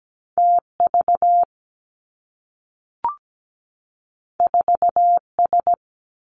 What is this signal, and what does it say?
Morse code, heard as a keyed tone, about 17 wpm, 700 Hz. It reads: T V 4 S
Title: Two Challenging Letter-Number Combinations 17wpm